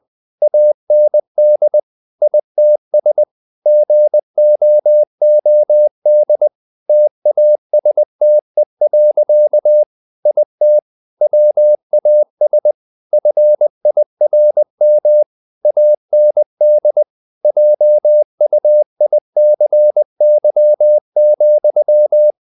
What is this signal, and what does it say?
Morse code, heard as a keyed tone, about 20 wpm, 600 Hz. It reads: AND ITS GOOD TASTE. IT WAS FIRM AND JUICY,
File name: the old man and the sea part 2 20wpm